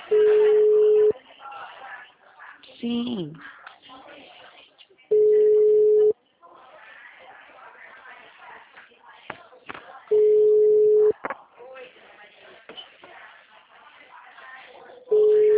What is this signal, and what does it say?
Ringback tone (Europe, 425 Hz), with speech.